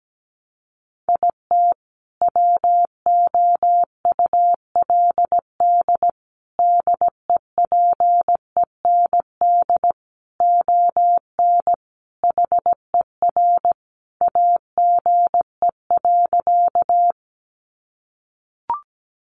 Morse code, heard as a keyed tone, about 17 wpm, 700 Hz. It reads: IT WOULD DEPEND ON HER AGE.
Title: Sentences from Top 1300 Words 17wpm